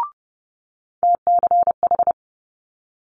Morse code, heard as a keyed tone, about 30 wpm, 700 Hz. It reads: TCH
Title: Multi-letter Phonemes Quick Fire 30wpm